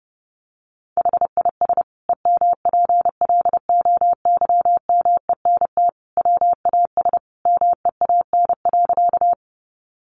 Morse code, heard as a keyed tone, about 30 wpm, 700 Hz. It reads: HIS EMPLOYMENT WAS MEAN.
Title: Sentences from Top 1500 Words 30wpm